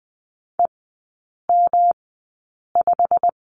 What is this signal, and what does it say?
Morse code, heard as a keyed tone, about 20 wpm, spaced as for 8 wpm, 700 Hz. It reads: EM5
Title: 121 Lesson 20e - Number 3 - Three Characters 20wpm